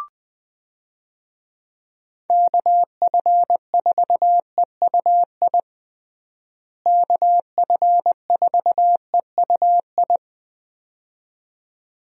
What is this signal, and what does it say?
Morse code, heard as a keyed tone, about 21 wpm, 700 Hz. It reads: KF4EUI KF4EUI
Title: USA Call Signs - 6 Characters - Repeated 2x - 20wpm